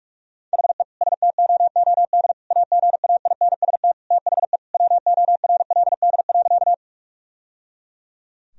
Morse code, heard as a keyed tone, about 41 wpm, 700 Hz. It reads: HE STOOD AGAINST THE WORLD.
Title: Sentences from Top 1000 Words - Review Part 2 - 45wpm